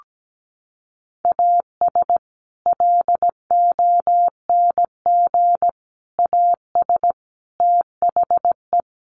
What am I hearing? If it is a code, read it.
Morse code, heard as a keyed tone, about 17 wpm, 700 Hz. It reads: AS LONG AS THE
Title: Sets of 4 Words 17wpm